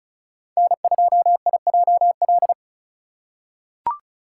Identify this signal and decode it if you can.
Morse code, heard as a keyed tone, about 36 wpm, 700 Hz. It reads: N2IJL
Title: USA Call Signs - 5 Characters - 2x Character Spacing - 35wpm